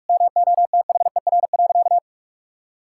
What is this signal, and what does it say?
Morse code, heard as a keyed tone, about 46 wpm, 700 Hz. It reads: MOTHER.
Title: Sentences from Top 1000 Words - Review Part 5 - 45wpm